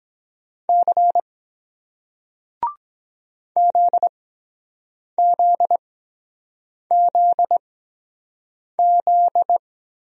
Morse code, heard as a keyed tone, about 23 wpm, spaced as for 7 wpm, 700 Hz. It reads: C ZZZZ